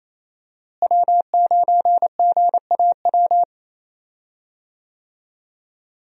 Morse code, heard as a keyed tone, about 28 wpm, 700 Hz. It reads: W9GAW
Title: USA Call Signs - 5 Characters - Repeated 2x - 28wpm